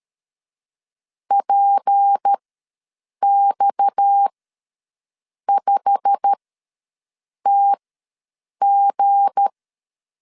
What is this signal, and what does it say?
Morse code, heard as a keyed tone, about 13 wpm, spaced as for 6 wpm, 800 Hz. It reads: PX5TG
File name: LEZIONI / 9 286 ws 12wpm 3351